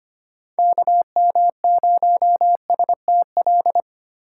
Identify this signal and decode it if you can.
Morse code, heard as a keyed tone, about 25 wpm, 700 Hz. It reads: KM0STL
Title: USA Call Signs - 6 Characters - 8x Character Spacing - 25wpm